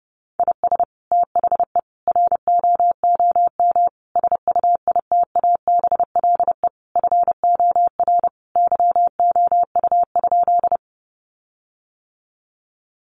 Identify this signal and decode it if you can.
Morse code, heard as a keyed tone, about 30 wpm, 700 Hz. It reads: IS THE ROOM SUITABLE FOR YOU?
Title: Sentences from Top 1400 Words 30wpm